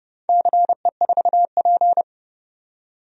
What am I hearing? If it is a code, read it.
Morse code, heard as a keyed tone, about 30 wpm, 700 Hz. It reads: CE4P
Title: DXCC-100 Call Signs - 4 Characters - 30wpm